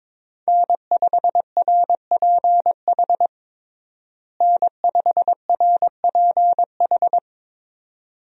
Morse code, heard as a keyed tone, about 22 wpm, 700 Hz. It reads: N5RPH N5RPH
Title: USA Call Signs - 5 Characters - Repeated 2x - 22wpm